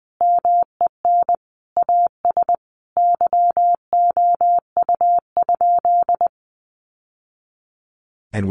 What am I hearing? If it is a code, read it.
Morse code, heard as a keyed tone, about 20 wpm, 700 Hz. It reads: MEN AS YOU?
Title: Sentences from Top 1000 Words 20wpm